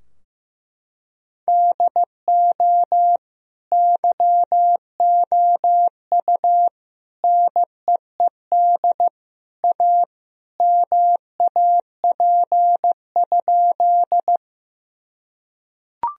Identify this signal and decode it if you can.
Morse code, heard as a keyed tone, about 15 wpm, 700 Hz. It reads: DO YOU NEED A MAP?
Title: Sentences from Top 1000 Words - Review Part 2 - 15wpm